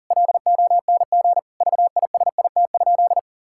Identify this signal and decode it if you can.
Morse code, heard as a keyed tone, about 40 wpm, 700 Hz. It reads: LONG VISIT?
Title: Sentences from Top 1000 Words - Review Part 6 - 40wpm